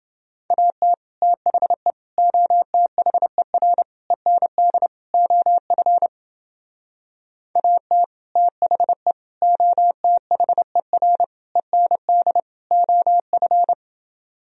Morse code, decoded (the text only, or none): AT THE OTHER END OF AT THE OTHER END OF